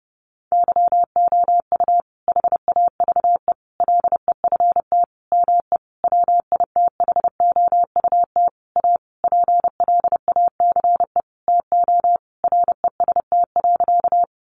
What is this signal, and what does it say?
Morse code, heard as a keyed tone, about 30 wpm, 700 Hz. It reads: YOU HAVE LEFT ME WITHOUT A PLACE TO REST.
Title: Sentences from Top 600 Words 30wpm